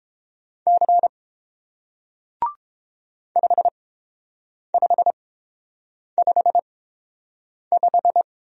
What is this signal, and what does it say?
Morse code, heard as a keyed tone, about 31 wpm, spaced as for 7 wpm, 700 Hz. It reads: C 5555